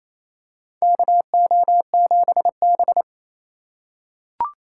Morse code, heard as a keyed tone, about 28 wpm, 700 Hz. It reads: KO7B
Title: USA Call Signs - 4 Characters - 3x Character Spacing - 28wpm